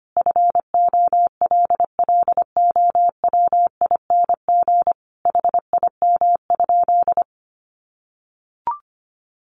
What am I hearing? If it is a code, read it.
Morse code, heard as a keyed tone, about 25 wpm, 700 Hz. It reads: FOLLOWING HIM?
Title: Sentences from Top 700 Words 25wpm